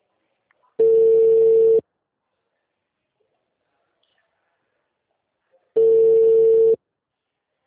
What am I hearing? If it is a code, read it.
Ringback tone (China, 450 Hz).